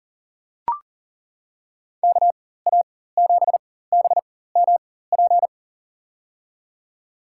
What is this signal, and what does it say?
Morse code, heard as a keyed tone, about 40 wpm, spaced as for 19 wpm, 700 Hz. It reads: KA7BMP
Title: USA Call Signs - 6 Characters - 4x Character Spacing - 40wpm